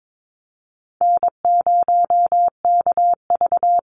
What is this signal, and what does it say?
Morse code, heard as a keyed tone, about 22 wpm, 700 Hz. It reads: N0KV
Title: Call Signs - 22wpm